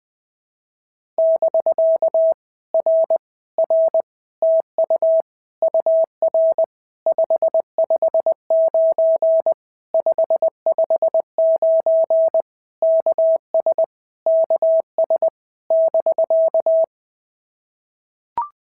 Morse code, heard as a keyed tone, about 20 wpm, 650 Hz. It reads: <BK> R R TU UR 559 559 KS KS <BK>
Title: POTA QSO Practice - 20wpm